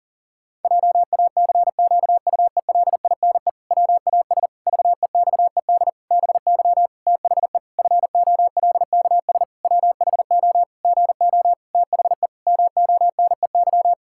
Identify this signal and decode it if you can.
Morse code, heard as a keyed tone, about 37 wpm, 700 Hz. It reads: JACQUELINE WAS VEXED BY THE FOLKS WHO GO THE MONEY
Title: Pangram Challenge v2 - 40wpm